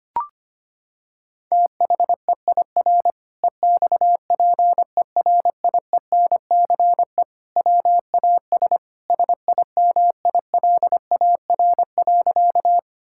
Morse code, heard as a keyed tone, about 25 wpm, 700 Hz. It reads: THEIR EXPERIENCE WAS SIMILAR.
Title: Sentences from Top 1000 Words - Review Part 5 - 25wpm